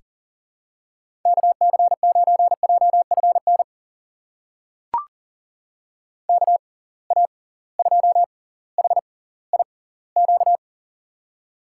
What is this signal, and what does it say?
Morse code, heard as a keyed tone, about 40 wpm, spaced as for 20 wpm, 700 Hz. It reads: KC9JFN KA2HIQ
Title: USA Call Signs - 6 Characters - 6x Character Spacing - 40wpm